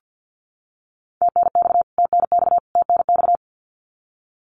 Morse code, heard as a keyed tone, about 50 wpm, 700 Hz. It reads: TNX TNX TNX
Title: Abbreviations Repeated 3x - 50wpm